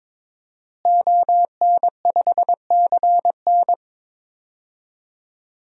Morse code, heard as a keyed tone, about 22 wpm, 700 Hz. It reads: ON5CN
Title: DXCC-100 Call Signs - 5 Characters - Repeated 3x - 22wpm